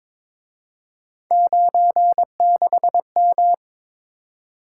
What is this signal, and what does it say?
Morse code, heard as a keyed tone, about 26 wpm, 700 Hz. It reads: 96M
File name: Speed-Racing - Three Letter-Number Combinations - 22wpm